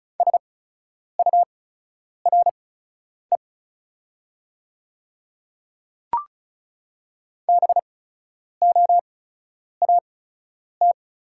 Morse code, heard as a keyed tone, about 34 wpm, spaced as for 10 wpm, 690 Hz. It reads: SURE BOAT
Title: 8x Character Spacing - Top 500 Words - 35wpm